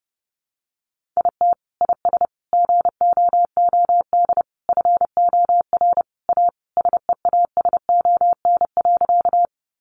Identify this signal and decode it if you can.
Morse code, heard as a keyed tone, about 30 wpm, 700 Hz. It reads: IT IS GOOD FOR A SEASON.
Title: Sentences from Top 1400 Words 30wpm